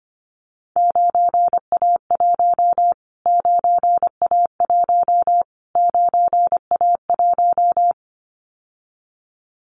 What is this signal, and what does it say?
Morse code, heard as a keyed tone, about 25 wpm, 700 Hz. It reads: 9A1 9A1 9A1
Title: Callsign Prefixes for Top 25 DXCC - Repeated 3x - 25wpm